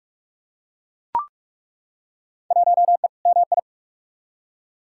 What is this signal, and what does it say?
Morse code, heard as a keyed tone, about 42 wpm, 700 Hz. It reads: 1E MI
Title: Field Day Exchanges - 45wpm